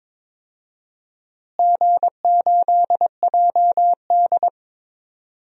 Morse code, heard as a keyed tone, about 22 wpm, 700 Hz. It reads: G8JD
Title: DXCC-100 Call Signs - 4 Characters - Repeated 2x - 22wpm